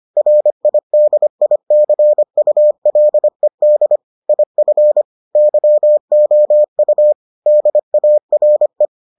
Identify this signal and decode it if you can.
Morse code, heard as a keyed tone, about 25 wpm, 600 Hz. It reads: RIDICULED IF YOU DARE